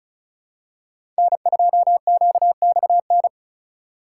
Morse code, heard as a keyed tone, about 34 wpm, 700 Hz. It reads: N2QXN
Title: USA Call Signs - 5 Characters - 2x Character Spacing - 35wpm